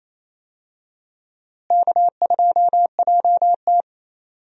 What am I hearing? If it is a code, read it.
Morse code, heard as a keyed tone, about 28 wpm, 700 Hz. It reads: K2JT
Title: USA Call Signs - 4 Characters - Repeated 3x - 28wpm